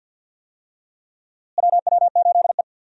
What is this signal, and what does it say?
Morse code, heard as a keyed tone, about 49 wpm, 700 Hz. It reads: WW8E
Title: USA Call Signs - 4 Characters - Repeated 3x - 50wpm